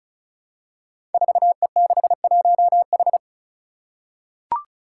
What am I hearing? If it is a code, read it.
Morse code, heard as a keyed tone, about 36 wpm, 700 Hz. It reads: 4E61H
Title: Five Challenging Letter-Number Combinations 35wpm